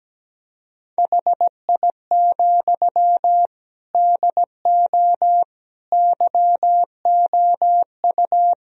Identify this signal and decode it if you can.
Morse code, heard as a keyed tone, about 17 wpm, 700 Hz. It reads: HI, DO YOU
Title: Sentences from Top 1000 Words - Review Part 3 - 17wpm